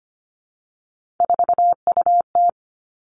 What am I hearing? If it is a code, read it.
Morse code, heard as a keyed tone, about 25 wpm, 700 Hz. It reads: 4UT
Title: Three Challenging Letter-Number Combinations 25wpm